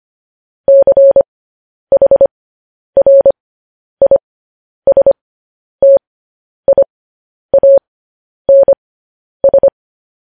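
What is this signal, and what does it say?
Morse code, heard as a keyed tone, about 25 wpm, spaced as for 10 wpm, 550 Hz. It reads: CHRISTIANS
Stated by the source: Morse code news headlines practice, at 10 WPM.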